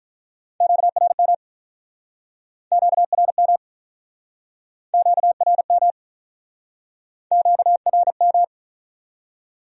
Morse code, heard as a keyed tone, about 45 wpm, 700 Hz. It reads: QRM QRM QRM QRM